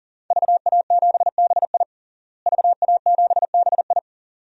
Morse code, heard as a keyed tone, about 40 wpm, 700 Hz. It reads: VA7BI VA7BI
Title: DXCC-100 Call Signs - 5 Characters - Repeated 2x - 40wpm